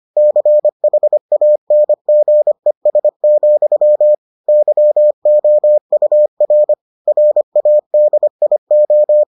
Morse code, heard as a keyed tone, about 25 wpm, 600 Hz. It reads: CHANGES, YOUR RADIO